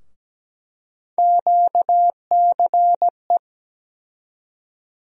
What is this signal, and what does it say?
Morse code, heard as a keyed tone, about 17 wpm, 700 Hz. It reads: QCE
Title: USA Call Sign Postfixes - 3 Letters 17wpm